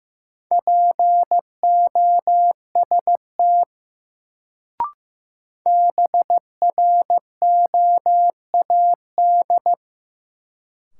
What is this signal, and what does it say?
Morse code, heard as a keyed tone, about 15 wpm, 700 Hz. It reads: POST BROAD
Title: Sentences from Top 1000 Words - New Word Drill - 15wpm